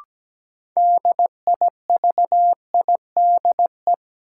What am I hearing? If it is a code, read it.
Morse code, heard as a keyed tone, about 17 wpm, 700 Hz. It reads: DIVIDE
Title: Sentences from Top 1600 Words - New Word Drill - 17wpm